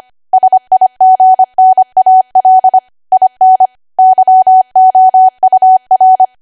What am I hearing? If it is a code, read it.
Morse code, heard as a keyed tone, about 25 wpm, 750 Hz. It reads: SIGNAL IN YOUR
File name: Archive / 25 WPM / 25WPM